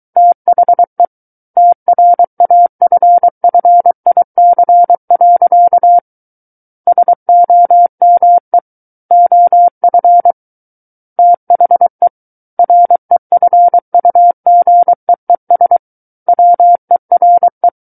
Morse code, heard as a keyed tone, about 23 wpm, 700 Hz. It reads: THE TRAFFIC. SOME OF THE REFUGEES WERE
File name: Book One / 14-Chapter 14 - 23 WPM